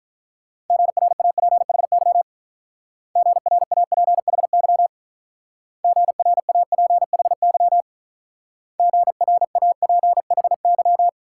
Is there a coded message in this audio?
Yes: Morse code from 0.7 to 11.1 s, heard as a keyed tone, about 44 wpm, 700 Hz. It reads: GRAPHY GRAPHY GRAPHY GRAPHY